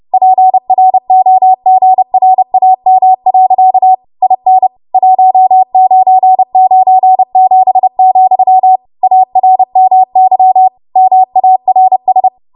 Morse code, heard as a keyed tone, about 30 wpm, 750 Hz. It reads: PROGRAM. IN 1997, ARMY MARS
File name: Archive / 30 WPM / 30WPM